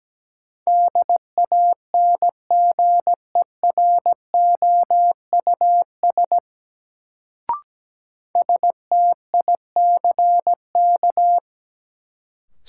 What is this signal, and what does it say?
Morse code, heard as a keyed tone, about 17 wpm, 700 Hz. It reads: DANGEROUS STICK
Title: Sentences from Top 1200 Words - New Word Drill Alt - 17wpm